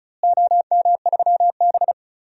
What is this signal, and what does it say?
Morse code, heard as a keyed tone, about 36 wpm, 700 Hz. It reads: OM3B
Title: DXCC-100 Call Signs - 4 Characters - Repeated 2x - 35wpm